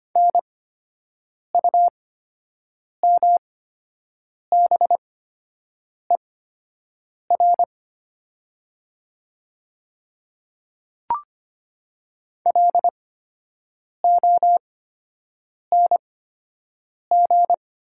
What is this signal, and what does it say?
Morse code, heard as a keyed tone, about 25 wpm, spaced as for 7 wpm, 700 Hz. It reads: NUMBER LONG
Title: 8x Character Spacing - Top 200 Words - 25wpm